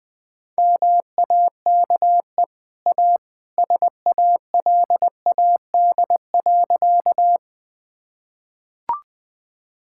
Morse code, heard as a keyed tone, about 21 wpm, 700 Hz. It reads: MAKE A SALAD.
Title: Easy Sentences with Hints - 20wpm